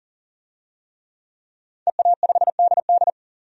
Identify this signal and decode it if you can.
Morse code, heard as a keyed tone, about 40 wpm, 700 Hz. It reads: EA5DD
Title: DXCC-100 Call Signs - 5 Characters - 40wpm